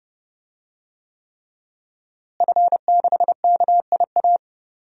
Morse code, heard as a keyed tone, about 30 wpm, 700 Hz. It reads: F6KIA
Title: DXCC-100 Call Signs - 5 Characters - Repeated 3x - 30wpm